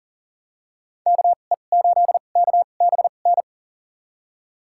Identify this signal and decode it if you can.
Morse code, heard as a keyed tone, about 39 wpm, spaced as for 29 wpm, 700 Hz. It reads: KE8KBN
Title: USA Call Signs - 6 Characters - 2x Character Spacing - 40wpm